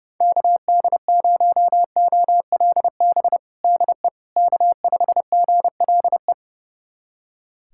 Morse code, heard as a keyed tone, about 29 wpm, 700 Hz. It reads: KD0OLB DE K5GLE
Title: USA Callsign DE Callsign 30wpm